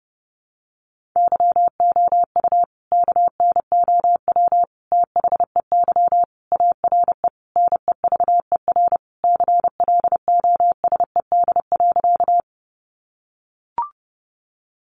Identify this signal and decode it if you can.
Morse code, heard as a keyed tone, about 30 wpm, 700 Hz. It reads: YOU KNOW THEY ARE NEVER CLOSED.
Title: Sentences from Top 1400 Words 30wpm